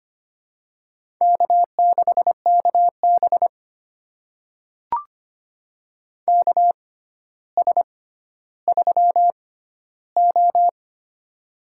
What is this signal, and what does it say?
Morse code, heard as a keyed tone, about 25 wpm, spaced as for 13 wpm, 700 Hz. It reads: K6KB KS3O
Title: USA Call Signs - 4 Characters - 6x Character Spacing - 25wpm